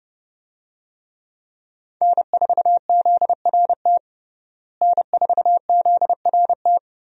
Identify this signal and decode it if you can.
Morse code, heard as a keyed tone, about 30 wpm, 700 Hz. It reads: N4ZRT N4ZRT
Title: USA Call Signs - 5 Characters - Repeated 3x - 30wpm